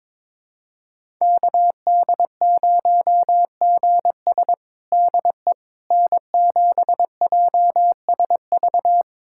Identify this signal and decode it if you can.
Morse code, heard as a keyed tone, about 22 wpm, 700 Hz. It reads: KD0GS DE N7JSV
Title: USA Callsign DE Callsign - 5 Characters - 22wpm